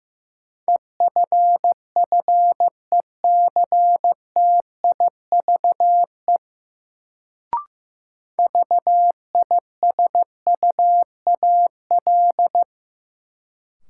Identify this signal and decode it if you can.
Morse code, heard as a keyed tone, about 15 wpm, 700 Hz. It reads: EFFECTIVE VISUAL
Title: Sentences from Top 1000 Words - Review Part 6 - New Word Drill - 15wpm